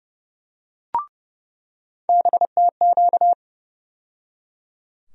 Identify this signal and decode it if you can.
Morse code, heard as a keyed tone, about 32 wpm, 700 Hz. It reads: BTQ
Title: USA Call Sign Postfixes - 3 Letters 30wpm